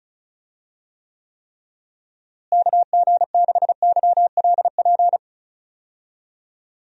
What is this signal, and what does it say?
Morse code, heard as a keyed tone, about 34 wpm, 700 Hz. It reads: KG6YLP
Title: USA Call Signs - 6 Characters - Repeated 3x - 35wpm